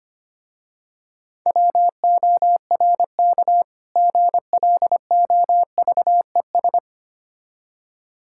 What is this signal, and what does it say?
Morse code, heard as a keyed tone, about 25 wpm, 700 Hz. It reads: WORK GLOVES
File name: practice / 25wpm / Difficult Words with Hints - 25wpm